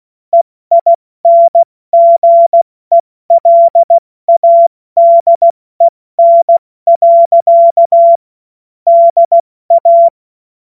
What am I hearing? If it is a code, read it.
Morse code, heard as a keyed tone, about 16 wpm, 680 Hz. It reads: EINGELADEN. DA
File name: Witz-05-August-2025-70-F80